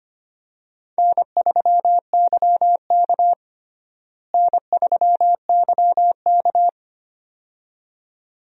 Morse code, heard as a keyed tone, about 25 wpm, 700 Hz. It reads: N3YK N3YK
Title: USA Call Signs - 4 Characters - Repeated 2x - 25wpm